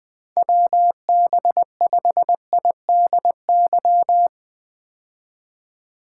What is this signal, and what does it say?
Morse code, heard as a keyed tone, about 20 wpm, 700 Hz. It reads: WB5IDY
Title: K1USN SST QSO Practice - 20wpm